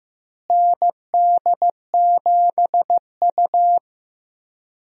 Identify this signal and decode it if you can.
Morse code, heard as a keyed tone, about 15 wpm, 700 Hz. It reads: ND7U
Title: USA Call Signs - 4 Characters - Repeated 3x - 15wpm